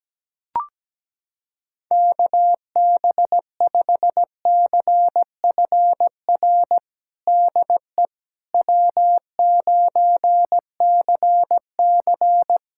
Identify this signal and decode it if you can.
Morse code, heard as a keyed tone, about 17 wpm, 700 Hz. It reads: KB5CFR DE W9CC
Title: USA Callsign DE Callsign 17wpm